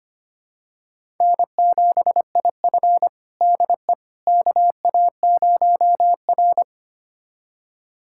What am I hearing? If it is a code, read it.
Morse code, heard as a keyed tone, about 25 wpm, 700 Hz. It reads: N7IF DE KA0R
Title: USA Callsign DE Callsign - 4 Characters - 25wpm